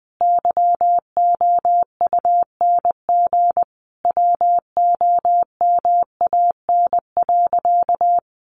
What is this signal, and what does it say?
Morse code, heard as a keyed tone, about 20 wpm, 700 Hz. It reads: YOUNG WOMAN.
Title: Sentences from Top 1100 Words 20wpm